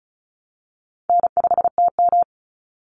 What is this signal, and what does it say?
Morse code, heard as a keyed tone, about 34 wpm, 700 Hz. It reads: N5TM
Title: Call Signs - 35wpm